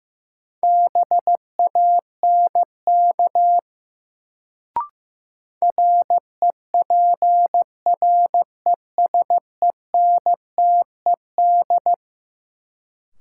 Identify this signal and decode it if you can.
Morse code, heard as a keyed tone, about 15 wpm, 700 Hz. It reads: BANK REPRESENTED
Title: Sets of 5 Words - New Word Drill - 15wpm